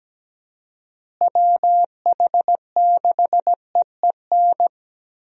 Morse code, heard as a keyed tone, about 17 wpm, 700 Hz. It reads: WH6EEN
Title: USA Call Signs - 6 Characters - Repeated 2x - 17wpm